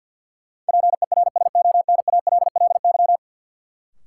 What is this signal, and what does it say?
Morse code, heard as a keyed tone, about 51 wpm, 700 Hz. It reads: PERSONALLY
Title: Sentences from Top 1400 Words - New Word Drill - 50wpm